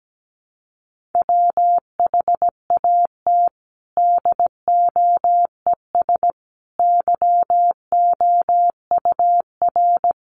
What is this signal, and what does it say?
Morse code, heard as a keyed tone, about 17 wpm, 700 Hz. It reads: WHAT DOES YOUR
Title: Sentences from Top 1400 Words 17wpm